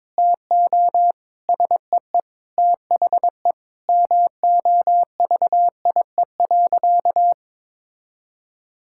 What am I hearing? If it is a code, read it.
Morse code, heard as a keyed tone, about 22 wpm, 700 Hz. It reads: TO SEE THE MOVIE.
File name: practice / 22wpm / Sentences with 3-Word Phrases - 22wpm